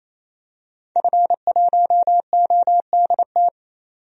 Morse code, heard as a keyed tone, about 27 wpm, 700 Hz. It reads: F1ODT
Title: DXCC-100 Call Signs - 5 Characters - 28wpm